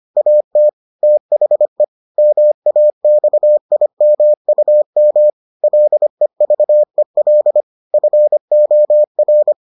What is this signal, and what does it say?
Morse code, heard as a keyed tone, about 25 wpm, 600 Hz. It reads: AT THE MAXIMUM LEVEL FOR